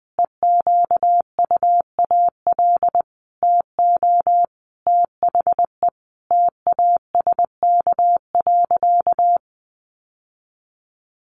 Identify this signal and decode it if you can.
Morse code, heard as a keyed tone, about 20 wpm, 700 Hz. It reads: EQUAL TO THE TASK.
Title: Sentences from Top 1400 Words 20wpm